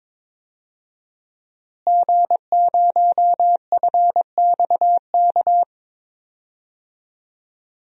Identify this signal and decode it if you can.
Morse code, heard as a keyed tone, about 22 wpm, 700 Hz. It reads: G0FXK
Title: DXCC-100 Call Signs - 5 Characters - Repeated 2x - 22wpm